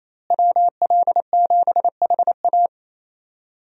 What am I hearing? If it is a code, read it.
Morse code, heard as a keyed tone, about 28 wpm, 700 Hz. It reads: WL7HA
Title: USA Call Signs - 5 Characters - Repeated 2x - 28wpm